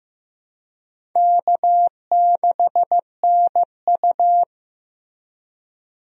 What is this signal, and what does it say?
Morse code, heard as a keyed tone, about 15 wpm, 700 Hz. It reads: K6NU
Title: USA Call Signs - 4 Characters - Repeated 2x - 15wpm